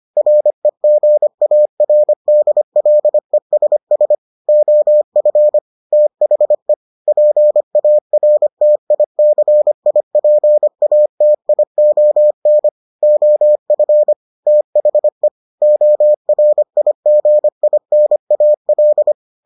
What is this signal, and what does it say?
Morse code, heard as a keyed tone, about 25 wpm, 600 Hz. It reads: REGARDLESS OF THE PARTICIPATION OF THE ORIGINAL